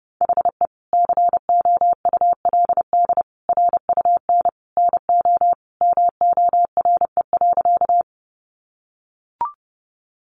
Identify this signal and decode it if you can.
Morse code, heard as a keyed tone, about 30 wpm, 700 Hz. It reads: HE COULD RUN NO MORE.
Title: Sentences from Top 300 Words 30wpm